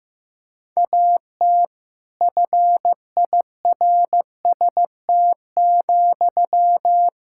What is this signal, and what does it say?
Morse code, heard as a keyed tone, about 15 wpm, 700 Hz. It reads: AT FIRST,
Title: Sentences from Top 1000 Words - Review Part 6 - 15wpm